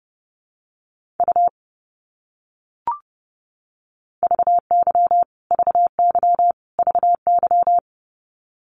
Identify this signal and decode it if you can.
Morse code, heard as a keyed tone, about 30 wpm, 700 Hz. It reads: U VY VY VY
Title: Abbreviations Repeated 3x - 30wpm